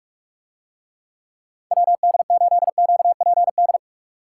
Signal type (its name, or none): Morse code